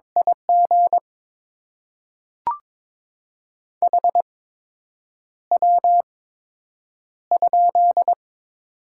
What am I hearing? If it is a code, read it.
Morse code, heard as a keyed tone, about 22 wpm, spaced as for 8 wpm, 700 Hz. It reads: IG HW?
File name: Top 20 QSO Elements - 8x Character Spacing - 22wpm